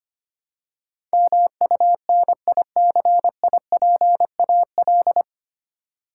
Morse code, heard as a keyed tone, about 25 wpm, 700 Hz. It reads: MUNICIPAL
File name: practice / 25wpm / 9 Letter Words - First 3 Letters Given - 25wpm